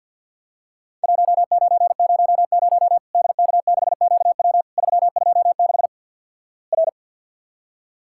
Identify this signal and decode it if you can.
Morse code, heard as a keyed tone, about 50 wpm, 700 Hz. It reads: 1900 DK6QW 326 R
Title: Worked All Europe DX Contest - QTC Exchange with Repeats - 50wpm